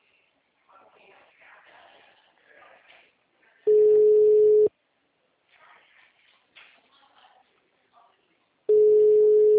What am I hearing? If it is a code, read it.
Ringback tone (Europe, 425 Hz).
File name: out- -2020- - - .33532